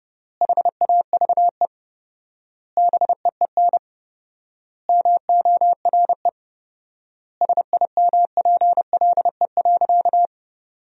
Morse code, heard as a keyed tone, about 32 wpm, 700 Hz. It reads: HAVE BEEN MORE SIMPLE.
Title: 4x Word-Spacing - Sentences from Top 400 Words - 30wpm